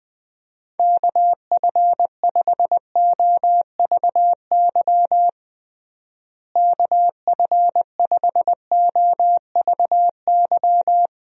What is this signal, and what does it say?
Morse code, heard as a keyed tone, about 21 wpm, 700 Hz. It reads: KF5OVY KF5OVY
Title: USA Call Signs - 6 Characters - Repeated 2x - 20wpm